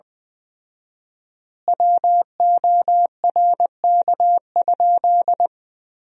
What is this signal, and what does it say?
Morse code, heard as a keyed tone, about 20 wpm, 700 Hz. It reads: WORK?
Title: 4x Word-Spacing - Sentences from Top 500 Words - 20wpm